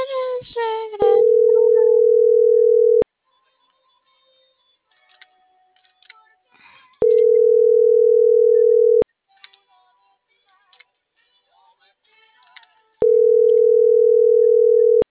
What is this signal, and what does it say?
Ringback tone (North America), 440 + 480 Hz.